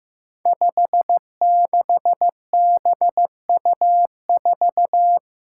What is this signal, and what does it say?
Morse code, heard as a keyed tone, about 15 wpm, 700 Hz. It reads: 56BU4
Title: Five Challenging Letter-Number Combinations 15wpm